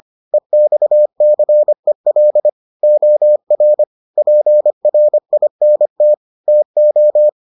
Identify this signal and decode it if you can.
Morse code, heard as a keyed tone, about 25 wpm, 600 Hz. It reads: EXCEL OR PRINT TO